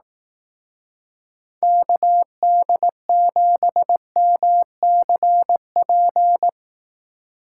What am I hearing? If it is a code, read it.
Morse code, heard as a keyed tone, about 18 wpm, 700 Hz. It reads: KD7MCP
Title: Speed Racing - USA 2x3 Call Signs 15wpm